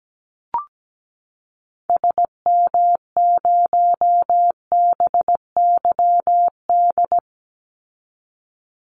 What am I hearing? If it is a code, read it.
Morse code, heard as a keyed tone, about 17 wpm, 700 Hz. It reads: SM0BYD
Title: Call Signs - 17wpm